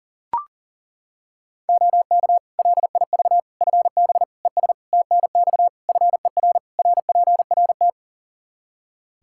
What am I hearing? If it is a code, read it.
Morse code, heard as a keyed tone, about 37 wpm, 700 Hz. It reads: OK LIV FB ES TNX FER RPRT
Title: Practice QSO - Exchange 2 - Info - 40wpm